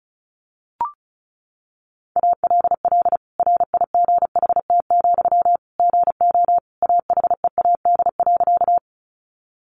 Morse code, heard as a keyed tone, about 34 wpm, 700 Hz. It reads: ALL RIGHT, GO AHEAD.
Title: Sentences from Top 1100 Words 35wpm